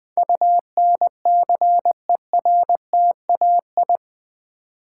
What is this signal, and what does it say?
Morse code, heard as a keyed tone, about 20 wpm, 700 Hz. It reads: UNCERTAI
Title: Build A Word - 4 Letters At A Time - 11-Letter Words - 20wpm